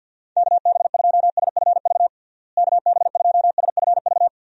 Morse code, heard as a keyed tone, about 50 wpm, 700 Hz. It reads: KB2SFV KB2SFV
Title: USA Call Signs - 6 Characters - Repeated 2x - 50wpm